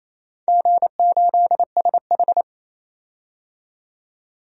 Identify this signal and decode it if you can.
Morse code, heard as a keyed tone, about 28 wpm, 700 Hz. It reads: G8SH
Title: DXCC-100 Call Signs - 4 Characters - 28wpm